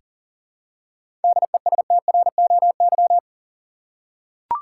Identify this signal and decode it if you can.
Morse code, heard as a keyed tone, about 39 wpm, 700 Hz. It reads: DESTROY
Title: Seven Letter Words - 40wpm